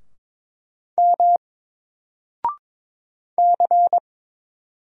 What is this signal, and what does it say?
Morse code, heard as a keyed tone, about 22 wpm, 700 Hz. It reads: M C
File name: Single-Letter-Number-22wpm